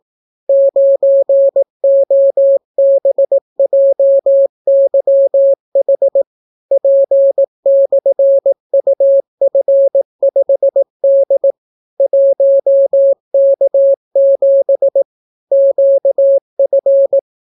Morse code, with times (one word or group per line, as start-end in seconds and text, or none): 0.5-6.2 9OBJYH
6.7-11.5 P/UF5D
12.0-15.0 1K7
15.5-17.2 QF